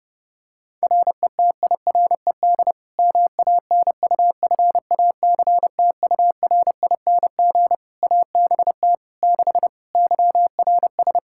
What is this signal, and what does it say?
Morse code, heard as a keyed tone, about 30 wpm, 700 Hz. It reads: RETIRED MANUFACTURING ABT 6 YRS
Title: Practice QSO - Exchange 3 - More Info - 30wpm